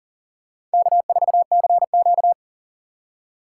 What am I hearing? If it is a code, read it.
Morse code, heard as a keyed tone, about 40 wpm, 700 Hz. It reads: K4CQ
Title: USA Call Signs - 4 Characters - 6x Character Spacing - 40wpm